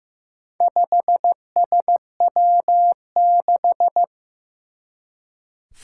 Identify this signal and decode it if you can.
Morse code, heard as a keyed tone, about 15 wpm, 700 Hz. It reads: 5SW6
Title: Four Challenging Letter-Number Combinations 15wpm